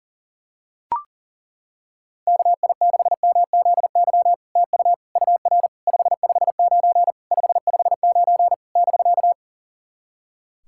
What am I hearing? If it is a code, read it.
Morse code, heard as a keyed tone, about 37 wpm, 700 Hz. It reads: KI6MZY TU UR 559 559 <BK>
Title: SOTA QSO Practice 40wpm